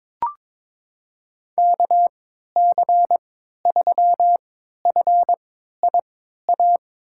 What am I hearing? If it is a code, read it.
Morse code, heard as a keyed tone, about 22 wpm, spaced as for 13 wpm, 700 Hz. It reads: KC3FIA
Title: USA Call Signs - 6 Characters - 3x Character Spacing - 22wpm